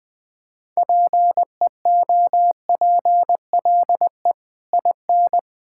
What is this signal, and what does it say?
Morse code, heard as a keyed tone, about 20 wpm, 700 Hz. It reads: PEOPLE IN
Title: Build A Phrase - 3-Word Phrases - 20wpm